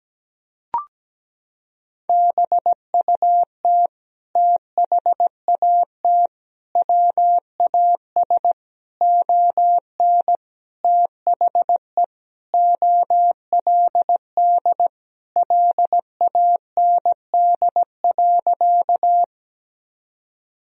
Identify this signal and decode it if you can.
Morse code, heard as a keyed tone, about 17 wpm, 700 Hz. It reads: BUT THAT WAS ON THE OLD LAND.
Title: Sentences from Top 1000 Words - Review Part 1 - 17wpm